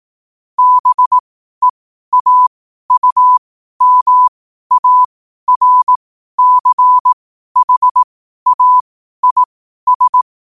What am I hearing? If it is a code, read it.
Morse code, heard as a keyed tone, about 18 wpm, spaced as for 12 wpm, 1000 Hz. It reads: BEAUMARCHAIS
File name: qotdfiles / 12wpm